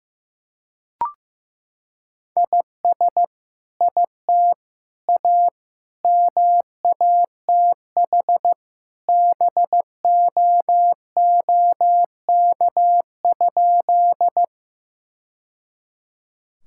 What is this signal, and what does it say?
Morse code, heard as a keyed tone, about 15 wpm, 700 Hz. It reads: IS IT A MATH BOOK?
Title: Sentences from Top 1000 Words - Review Part 6 - 15wpm